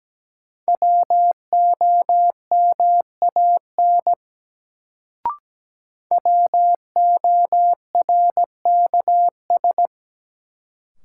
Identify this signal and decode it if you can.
Morse code, heard as a keyed tone, about 17 wpm, 700 Hz. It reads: WOMAN WORKS
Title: Sentences from Top 1100 Words - New Word Drill Alt - 17wpm